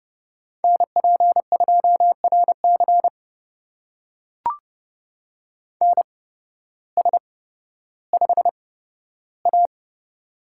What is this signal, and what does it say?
Morse code, heard as a keyed tone, about 29 wpm, spaced as for 14 wpm, 700 Hz. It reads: NP2RC NS5A
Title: USA Call Signs - 8x Character Spacing - 30wpm